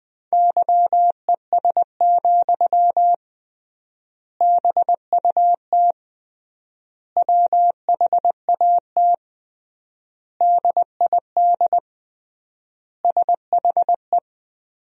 Morse code, heard as a keyed tone, about 21 wpm, 700 Hz. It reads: YES, BUT WHAT DID SHE
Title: 3x Word-Spacing - Sentences from Top 500 Words - 20wpm